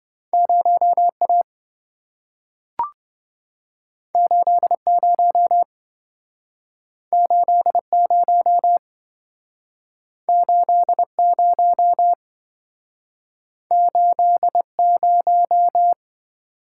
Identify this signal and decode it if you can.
Morse code, heard as a keyed tone, about 27 wpm, 700 Hz. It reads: 0A 80 80 80 80